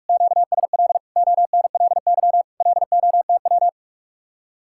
Morse code, heard as a keyed tone, about 49 wpm, 700 Hz. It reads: QSL ONLY LOTW